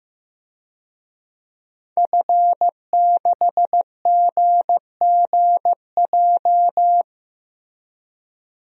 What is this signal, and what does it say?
Morse code, heard as a keyed tone, about 15 wpm, 700 Hz. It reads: F6GGJ
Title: DXCC-100 Call Signs - 5 Characters - Repeated 2x - 15wpm